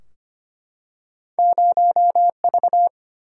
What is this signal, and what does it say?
Morse code, heard as a keyed tone, about 25 wpm, 700 Hz. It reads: 0 V
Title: Single-Letter Single-Number 25wpm